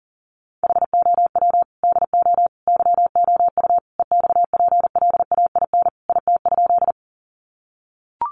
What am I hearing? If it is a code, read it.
Morse code, heard as a keyed tone, about 40 wpm, 700 Hz. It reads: HOW DO YOU EXPLAIN IT?
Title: Sentences from Top 1100 Words 40wpm